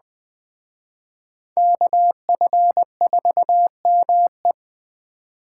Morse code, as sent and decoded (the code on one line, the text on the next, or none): -.- ..-. ....- -- .
KF4ME